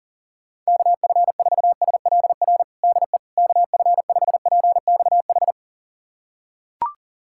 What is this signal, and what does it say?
Morse code, heard as a keyed tone, about 40 wpm, 700 Hz. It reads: KF4SLR DE KF5PXH
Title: USA Callsign DE Callsign - 6 Characters - 40wpm